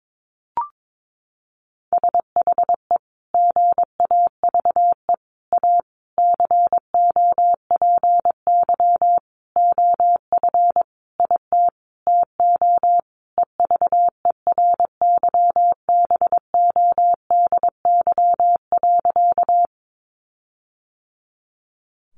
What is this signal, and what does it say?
Morse code, heard as a keyed tone, about 22 wpm, 700 Hz. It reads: SHE GAVE A COPY OF IT TO EVERYBODY.
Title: Sentences from Top 1400 Words 22wpm